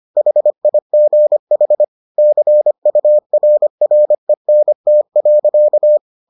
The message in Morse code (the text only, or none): HIGH CURRENT.